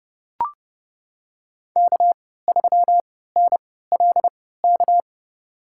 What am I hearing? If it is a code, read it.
Morse code, heard as a keyed tone, about 30 wpm, spaced as for 17 wpm, 690 Hz. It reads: K3NLK
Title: USA Call Signs - 3x Character Spacing - 30wpm